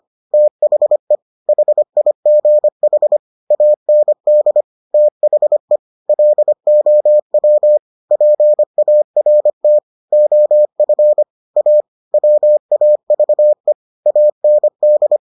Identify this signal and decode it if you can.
Morse code, heard as a keyed tone, about 25 wpm, 600 Hz. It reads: THE HIGH AND THE LOW PART OF A WAVE AND